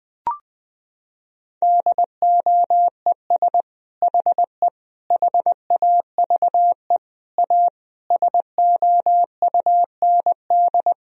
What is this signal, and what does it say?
Morse code, heard as a keyed tone, about 20 wpm, 700 Hz. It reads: DOES HE HAVE A SOUND
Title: Sentences from Top 1000 Words - Review Part 2 - 20wpm